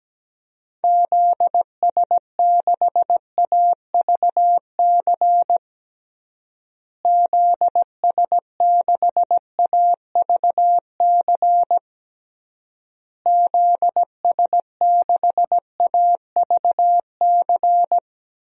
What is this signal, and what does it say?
Morse code, heard as a keyed tone, about 17 wpm, 700 Hz. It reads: ZS6AVC ZS6AVC ZS6AVC
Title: DXCC-100 Call Signs - 6 Characters - Repeated 3x - 17wpm